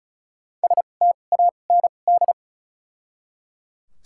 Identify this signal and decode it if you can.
Morse code, heard as a keyed tone, about 34 wpm, spaced as for 25 wpm, 700 Hz. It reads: STAND
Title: 2x Character Spacing - Top 300 Words - 35wpm